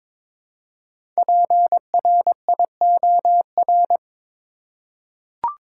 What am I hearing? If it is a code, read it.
Morse code, heard as a keyed tone, about 22 wpm, 700 Hz. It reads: PRIOR
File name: 5 Letter Words - First 3 Letters Given - 22wpm